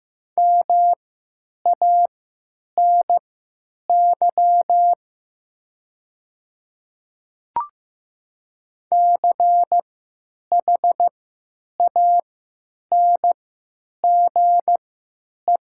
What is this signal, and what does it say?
Morse code, heard as a keyed tone, about 15 wpm, spaced as for 9 wpm, 700 Hz. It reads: MANY CHANGE
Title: 3x Character Spacing - Top 300 Words - 15wpm